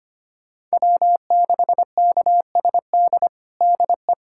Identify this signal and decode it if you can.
Morse code, heard as a keyed tone, about 25 wpm, 700 Hz. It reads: W6KSD DE
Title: USA Callsign DE Callsign - 5 Characters - 25wpm